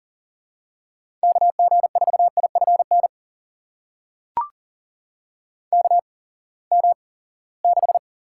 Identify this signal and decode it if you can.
Morse code, heard as a keyed tone, about 40 wpm, spaced as for 23 wpm, 700 Hz. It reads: KG4IFN KM6
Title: USA Call Signs - 6 Characters - 8x Character Spacing - 40wpm